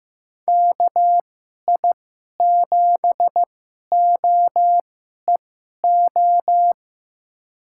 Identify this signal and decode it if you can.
Morse code, heard as a keyed tone, about 15 wpm, spaced as for 11 wpm, 700 Hz. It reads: KI7OEO
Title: USA Call Signs - 6 Characters - 2x Character Spacing - 15wpm